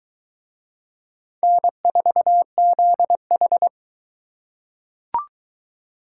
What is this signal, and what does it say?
Morse code, heard as a keyed tone, about 23 wpm, 700 Hz. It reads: N4ZH